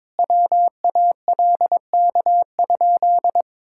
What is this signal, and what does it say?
Morse code, heard as a keyed tone, about 22 wpm, 700 Hz. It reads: WALK?
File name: practice / 22wpm / Sentences with 3-Word Phrases - 22wpm